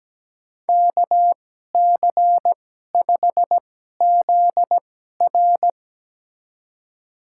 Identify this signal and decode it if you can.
Morse code, heard as a keyed tone, about 17 wpm, spaced as for 12 wpm, 700 Hz. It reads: KC5ZR
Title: USA Call Signs - 2x Character Spacing - 17wpm